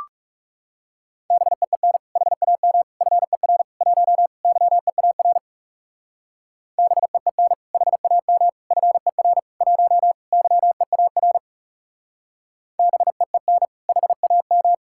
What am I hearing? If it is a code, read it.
Morse code, heard as a keyed tone, about 38 wpm, 700 Hz. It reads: BEEN HAM FER 1 YEAR BEEN HAM FER 1 YEAR BEEN HAM